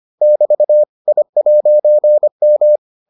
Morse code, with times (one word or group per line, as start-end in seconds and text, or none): 0.2-0.8 <BT>
1.1-2.8 I'M